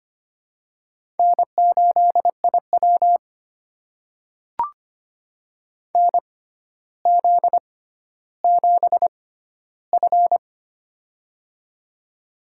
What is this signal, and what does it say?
Morse code, heard as a keyed tone, about 25 wpm, spaced as for 13 wpm, 700 Hz. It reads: N8IW NZ7F
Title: USA Call Signs - 4 Characters - 6x Character Spacing - 25wpm